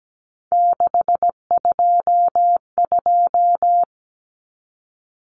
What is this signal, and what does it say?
Morse code, heard as a keyed tone, about 17 wpm, 700 Hz. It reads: 622
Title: Common Cut Numbers - 3-Digit 17wpm